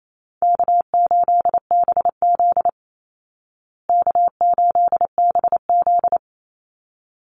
Morse code, heard as a keyed tone, about 28 wpm, 700 Hz. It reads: K8BZ K8BZ
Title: Easy Call Signs Repeated 2x - 28wpm